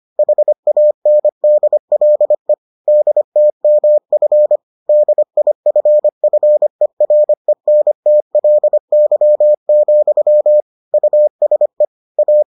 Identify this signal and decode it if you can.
Morse code, heard as a keyed tone, about 25 wpm, 600 Hz. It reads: HANDLE DTMF DIFFERENTLY, USE A